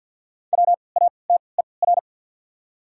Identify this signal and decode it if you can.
Morse code, heard as a keyed tone, about 51 wpm, spaced as for 28 wpm, 700 Hz. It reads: WATER
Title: 3x Character Spacing - Top 500 Words - 50wpm